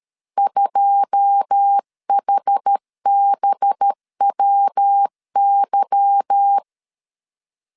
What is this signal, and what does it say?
Morse code, heard as a keyed tone, about 13 wpm, 800 Hz. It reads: 2HBWY
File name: CW / DET / 1 9 1500C S C 12wpm 3811